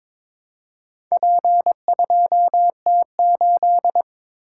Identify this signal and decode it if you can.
Morse code, heard as a keyed tone, about 22 wpm, 700 Hz. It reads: P2T8
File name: practice / 22wpm / Four Letter-Number Combinations - 22wpm